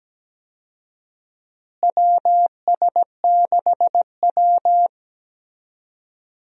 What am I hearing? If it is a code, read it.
Morse code, heard as a keyed tone, about 17 wpm, 700 Hz. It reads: WS6W
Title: USA Call Signs - 4 Characters - Repeated 3x - 17wpm